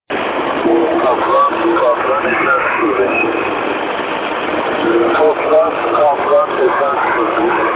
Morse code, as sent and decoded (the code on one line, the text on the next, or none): none